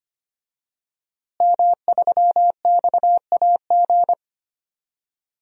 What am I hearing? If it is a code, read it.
Morse code, heard as a keyed tone, about 25 wpm, 700 Hz. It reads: M3XAG
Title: DXCC-100 Call Signs - 5 Characters - Repeated 3x - 25wpm